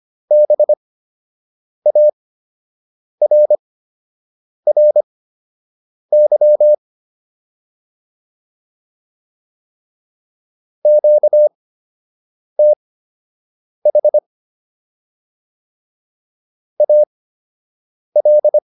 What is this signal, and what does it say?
Morse code, heard as a keyed tone, about 25 wpm, spaced as for 7 wpm, 600 Hz. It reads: BARRY QTH AL